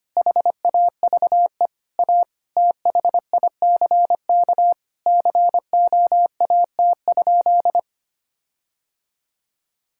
Morse code, heard as a keyed tone, about 25 wpm, 700 Hz. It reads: HAVE A THICK COAT?
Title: Sentences from Top 1600 Words 25wpm